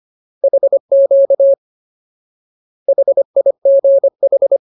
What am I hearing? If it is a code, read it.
Morse code, heard as a keyed tone, about 25 wpm, 550 Hz. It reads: HQ HIGH
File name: Ultimate Word List / 25WPM Ultimate